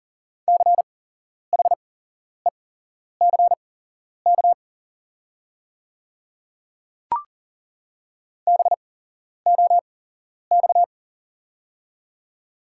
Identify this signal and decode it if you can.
Morse code, heard as a keyed tone, about 40 wpm, spaced as for 11 wpm, 690 Hz. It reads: CHECK BOX
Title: 8x Character Spacing - Top 500 Words - 40wpm